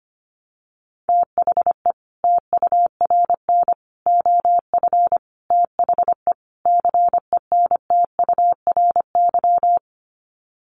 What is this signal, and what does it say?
Morse code, heard as a keyed tone, about 25 wpm, 700 Hz. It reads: THE TURN OF THE CENTURY
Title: Sets of 5 Words 25wpm